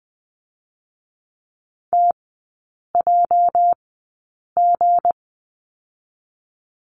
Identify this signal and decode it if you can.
Morse code, heard as a keyed tone, about 20 wpm, spaced as for 8 wpm, 700 Hz. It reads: TJG
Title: 218 Lesson 35e - Number 0 - Three Characters 20wpm